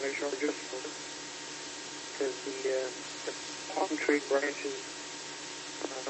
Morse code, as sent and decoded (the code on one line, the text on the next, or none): none